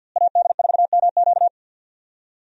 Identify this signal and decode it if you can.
Morse code, heard as a keyed tone, about 51 wpm, 700 Hz. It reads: AD4MQ
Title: USA Call Signs - 5 Characters - Repeated 3x - 50wpm